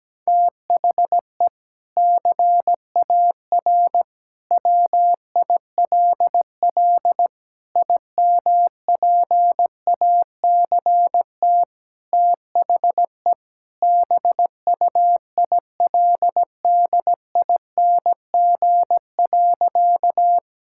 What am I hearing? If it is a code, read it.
Morse code, heard as a keyed tone, about 17 wpm, 700 Hz. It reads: THE CAR WILL IMPACT THE BUILDING.
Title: Sentences from Top 1000 Words - Review Part 6 - 17wpm